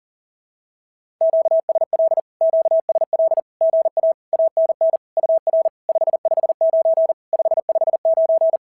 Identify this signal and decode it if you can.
Morse code, heard as a keyed tone, about 40 wpm, 650 Hz. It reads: QSL QSL GA ANN UR 559 559
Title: SOTA QSO Practice 40wpm